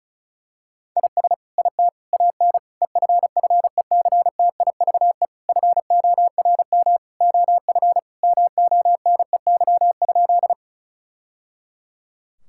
Morse code, heard as a keyed tone, about 36 wpm, 700 Hz. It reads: IS IT AN EFFECTIVE FORM OF MONEY?
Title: Sentences from Top 1000 Words - Review Part 6 - 35wpm